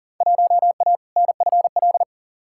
Morse code, heard as a keyed tone, about 40 wpm, 700 Hz. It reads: 1A NFL
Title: Field Day Exchanges - 40wpm